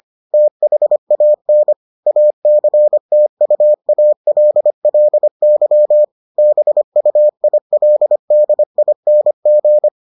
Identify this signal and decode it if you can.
Morse code, heard as a keyed tone, about 25 wpm, 600 Hz. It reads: THAN ACTUALLY BUILDING